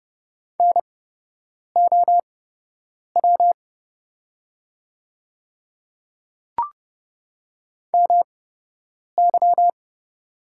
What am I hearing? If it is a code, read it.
Morse code, heard as a keyed tone, about 32 wpm, spaced as for 8 wpm, 700 Hz. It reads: NOW MY
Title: 8x Character Spacing - Top 100 Words - 30wpm